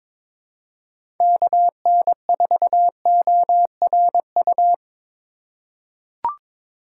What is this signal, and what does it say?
Morse code, heard as a keyed tone, about 22 wpm, 700 Hz. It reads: KN4ORU
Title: USA Call Signs - 6 Characters - 3x Character Spacing - 22wpm